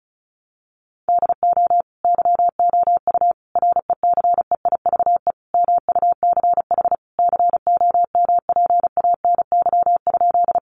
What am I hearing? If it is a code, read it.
Morse code, heard as a keyed tone, about 34 wpm, 700 Hz. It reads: DO YOU RECEIVE MUCH COMPANY?
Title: Sentences from Top 1000 Words 35wpm